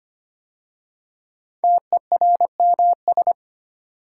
Morse code, heard as a keyed tone, about 25 wpm, 700 Hz. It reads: TERMS
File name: 5 Letter Words - First 3 Letters Given - 25wpm